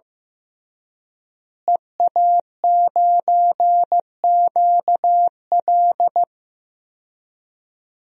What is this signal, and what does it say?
Morse code, heard as a keyed tone, about 15 wpm, 700 Hz. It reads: EA9QL
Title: DXCC-100 Call Signs - 5 Characters - Repeated 3x - 15wpm